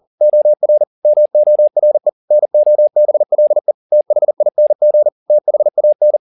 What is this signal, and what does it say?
Morse code, heard as a keyed tone, about 40 wpm, 600 Hz. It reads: OR MORE NOBLE THING THAN
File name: the old man and the sea 2 40wpm